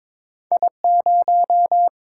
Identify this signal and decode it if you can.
Morse code, heard as a keyed tone, about 23 wpm, 700 Hz. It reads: I0
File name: practice / 22wpm / Two Letter-Number Combinations - 22wpm